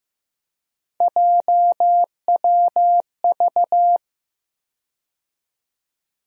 Morse code, heard as a keyed tone, about 16 wpm, 700 Hz. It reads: JWV
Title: USA Call Sign Postfixes - 3 Letters 15wpm